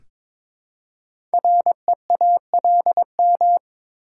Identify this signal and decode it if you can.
Morse code, heard as a keyed tone, about 22 wpm, 700 Hz. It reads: REALM
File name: 5 Letter Words - First 3 Letters Given - 22wpm